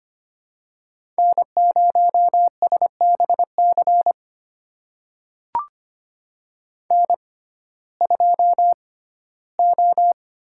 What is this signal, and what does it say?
Morse code, heard as a keyed tone, about 25 wpm, spaced as for 15 wpm, 700 Hz. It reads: N0SBC N2O
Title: USA Call Signs - 5 Characters - 6x Character Spacing - 25wpm